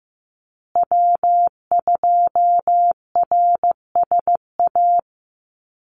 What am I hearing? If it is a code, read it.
Morse code, heard as a keyed tone, about 15 wpm, 700 Hz. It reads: W2RSA
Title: USA Call Signs Repeated 3x 15wpm